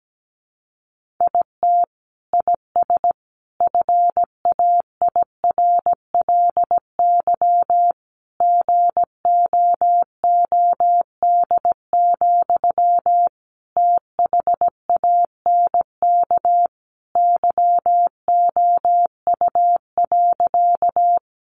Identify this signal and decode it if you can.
Morse code, heard as a keyed tone, about 17 wpm, 700 Hz. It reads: IT IS FAIRLY GOOD, THANK YOU.
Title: Sentences from Top 1100 Words 17wpm